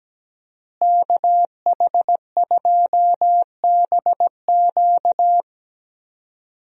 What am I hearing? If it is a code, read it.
Morse code, heard as a keyed tone, about 17 wpm, 700 Hz. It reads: KH2BQ
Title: USA Call Signs - 5 Characters - Repeated 2x - 17wpm